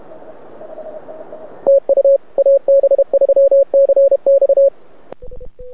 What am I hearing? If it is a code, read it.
Morse code, heard as a keyed tone, about 29 wpm, 550 Hz. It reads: TU AB3CX
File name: rec contest / arrl cw 2018 / test 32